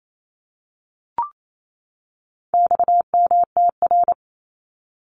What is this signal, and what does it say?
Morse code, heard as a keyed tone, about 28 wpm, 700 Hz. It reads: XMTR
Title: Abbreviations 28wpm